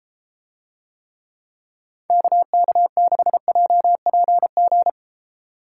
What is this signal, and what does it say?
Morse code, heard as a keyed tone, about 32 wpm, 690 Hz. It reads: KK6JPG